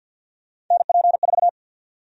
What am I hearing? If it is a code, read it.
Morse code, heard as a keyed tone, about 52 wpm, 700 Hz. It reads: NP4
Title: USA Call Sign Prefixes - 50wpm